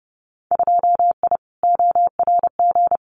Morse code, heard as a keyed tone, about 30 wpm, 700 Hz. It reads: 2I ORG
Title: Winter Field Day Exchanges Encore 30wpm